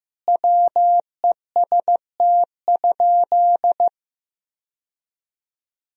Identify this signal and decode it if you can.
Morse code, heard as a keyed tone, about 15 wpm, 700 Hz. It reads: WEST?
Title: Sentences from Top 1000 Words - Review Part 2 - 15wpm